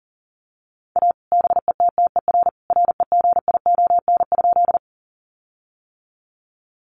Morse code, heard as a keyed tone, about 37 wpm, 700 Hz. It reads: A BETTER REGION?
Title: Sentences from Top 1500 Words 40wpm